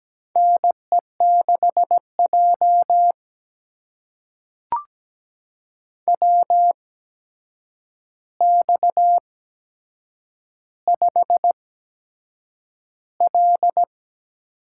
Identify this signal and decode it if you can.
Morse code, heard as a keyed tone, about 17 wpm, spaced as for 7 wpm, 700 Hz. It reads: NE6J WX5L
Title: USA Call Signs - 4 Characters - 8x Character Spacing - 17wpm